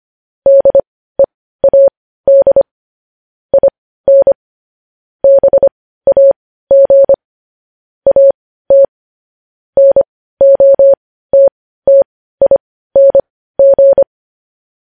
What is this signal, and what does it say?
Morse code, heard as a keyed tone, about 25 wpm, spaced as for 15 wpm, 550 Hz. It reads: DEAD IN BAG AT NOTTING